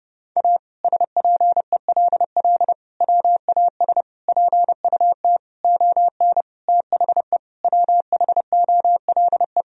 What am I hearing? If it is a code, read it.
Morse code, heard as a keyed tone, about 30 wpm, 700 Hz. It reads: A SPELL WAS PUT ON THE WHOLE
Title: Sentences from Top 1000 Words - Review Part 1 - 30wpm